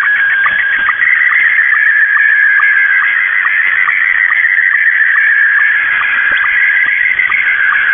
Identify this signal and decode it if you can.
SSTV (slow-scan television), Scottie 1.
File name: mw / 0256 6950u wolverine radio inc sstv